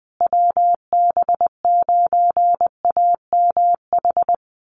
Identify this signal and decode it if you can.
Morse code, heard as a keyed tone, about 20 wpm, 700 Hz. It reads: WB9AMH
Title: USA Call Signs Repeated 3x 20wpm